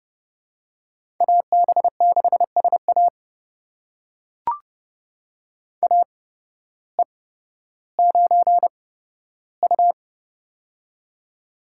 Morse code, heard as a keyed tone, about 30 wpm, spaced as for 14 wpm, 700 Hz. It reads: AB6SA AE9U
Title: USA Call Signs - 8x Character Spacing - 30wpm